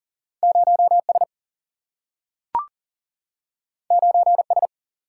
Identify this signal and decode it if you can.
Morse code, heard as a keyed tone, about 40 wpm, 700 Hz. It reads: 0S 9S
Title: Single-Letter Single-Number 40wpm